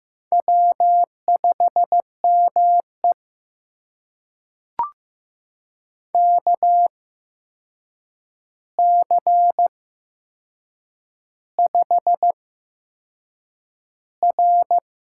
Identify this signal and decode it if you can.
Morse code, heard as a keyed tone, about 15 wpm, spaced as for 6 wpm, 700 Hz. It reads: W5ME KC5R
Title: USA Call Signs - 8x Character Spacing - 15wpm